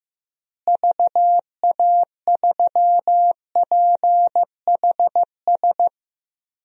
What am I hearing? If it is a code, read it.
Morse code, heard as a keyed tone, about 15 wpm, 700 Hz. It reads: VA3PHS
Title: DXCC-100 Call Signs - 6 Characters - Repeated 3x - 15wpm